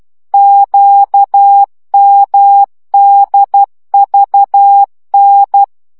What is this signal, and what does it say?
Morse code, heard as a keyed tone, about 12 wpm, 800 Hz. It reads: QMDVN
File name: morze vizsga 2011-2 10